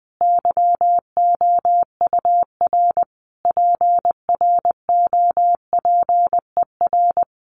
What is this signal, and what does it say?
Morse code, heard as a keyed tone, about 20 wpm, 700 Hz. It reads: YOUR PROPER
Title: Sentences from Top 800 Words 20wpm